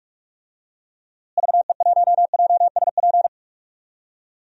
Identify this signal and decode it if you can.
Morse code, heard as a keyed tone, about 46 wpm, 700 Hz. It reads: VE1JSP
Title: DXCC-100 Call Signs - 6 Characters - Repeated 2x - 45wpm